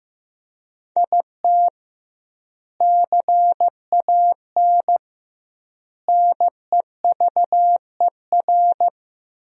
Morse code, heard as a keyed tone, about 15 wpm, 700 Hz. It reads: IT CAN NEVER
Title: 2x Word-Spacing - Sentences from Top 400 Words - 15wpm